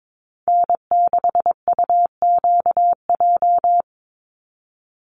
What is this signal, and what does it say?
Morse code, heard as a keyed tone, about 22 wpm, 700 Hz. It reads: N6UQJ
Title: USA Call Signs 22wpm